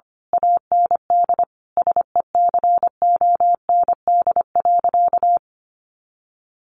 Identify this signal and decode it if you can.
Morse code, heard as a keyed tone, about 25 wpm, 700 Hz. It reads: AND SECOND.
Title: A Sound Of Thunder 25wpm